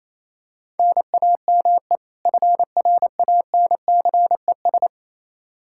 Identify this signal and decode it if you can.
Morse code, heard as a keyed tone, about 28 wpm, 700 Hz. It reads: NAME FRANCES
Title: Practice QSO - Exchange 1 - Beginning - 28wpm